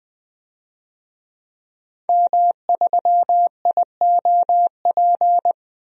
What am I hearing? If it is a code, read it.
Morse code, heard as a keyed tone, about 20 wpm, 700 Hz. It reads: M3IOP
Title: DXCC-100 Call Signs - 5 Characters - 20wpm